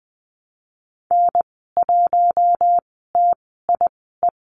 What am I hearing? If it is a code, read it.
Morse code, heard as a keyed tone, about 20 wpm, spaced as for 14 wpm, 700 Hz. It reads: N1TIE
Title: 042 Lesson 8g - Number 1 - Call Signs 20wpm